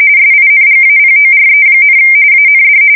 RTTY, about 45.45 baud. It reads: SINGAPORE, AND
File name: fsk170 / battleshipnj